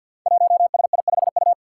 Morse code, heard as a keyed tone, about 51 wpm, 700 Hz. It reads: 1SI5U
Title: Five Challenging Letter-Number Combinations 50wpm